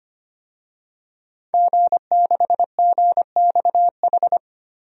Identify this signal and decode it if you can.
Morse code, heard as a keyed tone, about 25 wpm, 700 Hz. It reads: G6GXH
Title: DXCC-100 Call Signs - 5 Characters - 25wpm